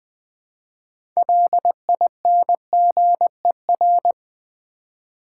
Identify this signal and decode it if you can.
Morse code, heard as a keyed tone, about 20 wpm, 700 Hz. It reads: LINGER
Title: 6 Letter Words - First 3 Letters Given - 20wpm